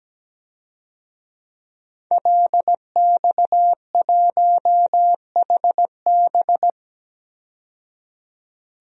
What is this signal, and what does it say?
Morse code, heard as a keyed tone, about 17 wpm, 700 Hz. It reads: LX1HB
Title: DXCC-100 Call Signs - 5 Characters - 17wpm